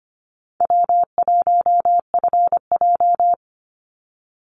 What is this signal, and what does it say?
Morse code, heard as a keyed tone, about 25 wpm, 700 Hz. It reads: W1FJ
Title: Easy Call Signs Repeated 3x - 25wpm